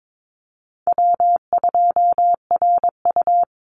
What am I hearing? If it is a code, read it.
Morse code, heard as a keyed tone, about 22 wpm, 700 Hz. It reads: W2RU
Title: Easy Call Signs - 22wpm